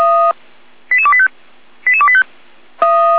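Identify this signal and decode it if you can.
Five-tone selective call (ZVEI), heard as quick runs of five tones, tones of about 70 ms. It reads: 88285 88285
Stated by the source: Five-tone ZVEI selective call received on an SDR.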